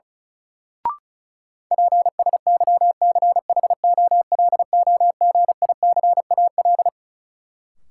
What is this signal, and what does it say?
Morse code, heard as a keyed tone, about 36 wpm, 700 Hz. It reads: PSYCHOLOGICAL
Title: Sentences from Top 1400 Words - New Word Drill - 35wpm